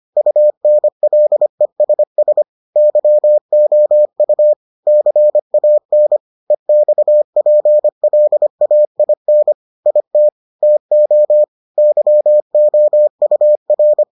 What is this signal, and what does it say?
Morse code, heard as a keyed tone, about 25 wpm, 600 Hz. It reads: UNLESS YOU CAN EXPLAIN IT TO YOUR